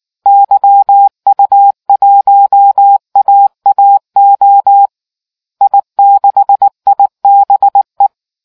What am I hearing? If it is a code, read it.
Morse code, heard as a keyed tone, about 19 wpm, 800 Hz. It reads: YU1AAO I6IBE